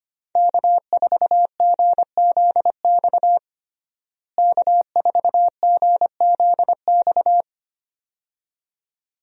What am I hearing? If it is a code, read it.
Morse code, heard as a keyed tone, about 25 wpm, 700 Hz. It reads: K4GZX K4GZX
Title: USA Call Signs - 5 Characters - Repeated 2x - 25wpm